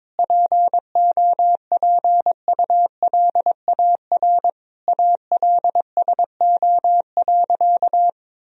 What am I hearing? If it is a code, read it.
Morse code, heard as a keyed tone, about 22 wpm, 700 Hz. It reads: POPULAR ALSO.
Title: Sentences from Top 1000 Words - Review Part 3 - 22wpm